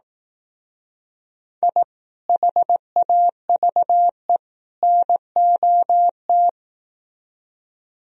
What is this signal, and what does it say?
Morse code, heard as a keyed tone, about 17 wpm, 700 Hz. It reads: I HAVE NOT
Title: Speed-Racing - Sets of 3 Words 15wpm